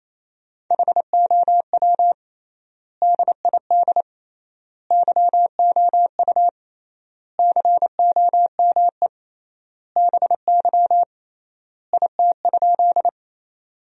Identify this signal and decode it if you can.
Morse code, heard as a keyed tone, about 28 wpm, 700 Hz. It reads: HOW DID YOU COME BY IT?
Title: 3x Word-Spacing - Sentences from Top 100 Words - 28wpm